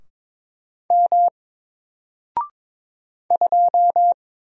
Morse code, heard as a keyed tone, about 22 wpm, 700 Hz. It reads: M 2
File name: Single-Letter-Number-22wpm